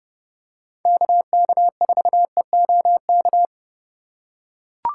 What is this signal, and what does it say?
Morse code, heard as a keyed tone, about 31 wpm, 700 Hz. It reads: KK4EOK
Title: USA Call Signs - 6 Characters - 4x Character Spacing - 30wpm